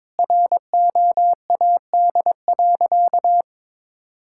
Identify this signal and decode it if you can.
Morse code, heard as a keyed tone, about 22 wpm, 700 Hz. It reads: ROAD.
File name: practice / 22wpm / Sentences with 4-Word Phrases - 22wpm